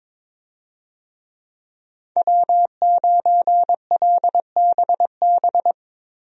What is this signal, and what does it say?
Morse code, heard as a keyed tone, about 22 wpm, 700 Hz. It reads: W9LBB
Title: USA Call Signs - 5 Characters - Repeated 3x - 22wpm